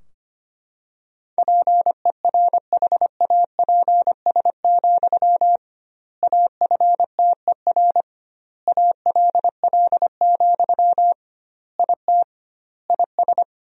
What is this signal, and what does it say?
Morse code, heard as a keyed tone, about 25 wpm, 700 Hz. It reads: PERHAPS, AFTER ALL, IT IS
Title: 2x Word-Spacing - Sentences from Top 500 Words - 25wpm